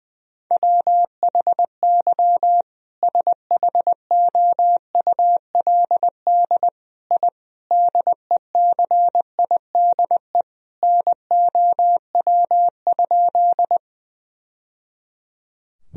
Morse code, heard as a keyed tone, about 20 wpm, 700 Hz. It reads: WHY SHOULD I DECIDE NOW?
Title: Sentences from Top 1000 Words - Review Part 3 - 20wpm